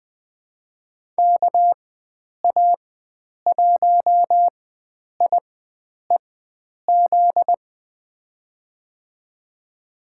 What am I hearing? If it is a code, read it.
Morse code, heard as a keyed tone, about 20 wpm, spaced as for 9 wpm, 700 Hz. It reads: KA1IEZ
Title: USA Call Signs - 6 Characters - 4x Character Spacing - 20wpm